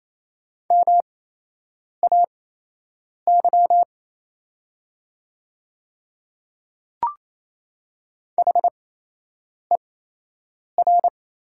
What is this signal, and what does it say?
Morse code, heard as a keyed tone, about 28 wpm, spaced as for 8 wpm, 700 Hz. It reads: MAY HER
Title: 8x Character Spacing - Top 100 Words - 28wpm